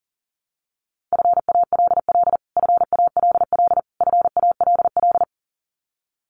Morse code, heard as a keyed tone, about 40 wpm, 700 Hz. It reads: FALL FALL FALL
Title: Top 500 Words Repeated 40wpm